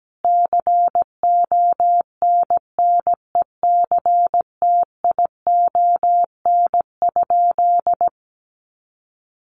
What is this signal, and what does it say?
Morse code, heard as a keyed tone, about 17 wpm, 700 Hz. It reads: CONNECTION?
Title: Sentences from Top 1300 Words 17wpm